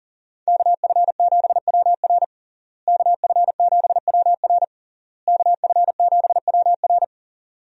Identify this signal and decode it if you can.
Morse code, heard as a keyed tone, about 40 wpm, 700 Hz. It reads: KF7WR KF7WR KF7WR
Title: USA Call Signs - 5 Characters - Repeated 3x - 40wpm